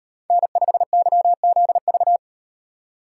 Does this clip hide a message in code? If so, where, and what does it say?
Yes: Morse code from 0.3 to 2.2 s, heard as a keyed tone, about 35 wpm, 700 Hz. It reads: N5YZV